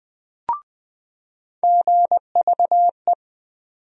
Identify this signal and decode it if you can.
Morse code, heard as a keyed tone, about 20 wpm, 700 Hz. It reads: GVE
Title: USA Call Sign Postfixes - 3 Letters 20wpm